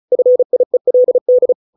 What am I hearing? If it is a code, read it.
Morse code, heard as a keyed tone, about 34 wpm, 500 Hz. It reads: FIELD